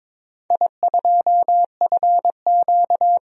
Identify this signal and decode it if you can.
Morse code, heard as a keyed tone, about 22 wpm, 700 Hz. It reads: I2FQ
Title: DXCC-100 Call Signs - 4 Characters - 22wpm